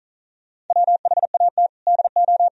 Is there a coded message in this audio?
Yes: Morse code from 0.7 to 2.5 s, heard as a keyed tone, about 47 wpm, 700 Hz. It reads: WHAT DO